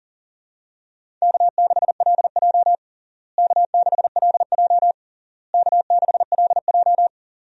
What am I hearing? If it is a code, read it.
Morse code, heard as a keyed tone, about 40 wpm, 700 Hz. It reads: K6LJ K6LJ K6LJ
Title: USA Call Signs - 4 Characters - Repeated 3x - 40wpm